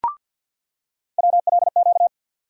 Morse code, heard as a keyed tone, about 46 wpm, 700 Hz. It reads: WLQ
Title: USA Call Sign Postfixes - 3 Letters 50wpm